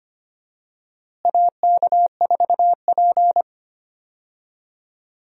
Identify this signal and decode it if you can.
Morse code, heard as a keyed tone, about 25 wpm, 700 Hz. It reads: AK4P
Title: USA Call Signs - 4 Characters - Repeated 2x - 25wpm